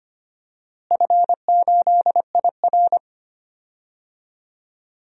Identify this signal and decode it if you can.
Morse code, heard as a keyed tone, about 25 wpm, 700 Hz. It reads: F8IR
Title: DXCC-100 Call Signs - 4 Characters - Repeated 3x - 25wpm